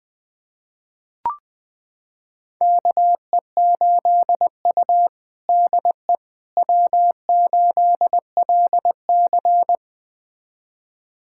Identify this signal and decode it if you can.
Morse code, heard as a keyed tone, about 20 wpm, 700 Hz. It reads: KE8U DE W8LC
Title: USA Callsign DE Callsign - 4 Characters - 20wpm